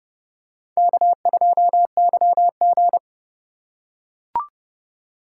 Morse code, heard as a keyed tone, about 30 wpm, 700 Hz. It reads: K2YG
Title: USA Call Signs - 4 Characters - 4x Character Spacing - 30wpm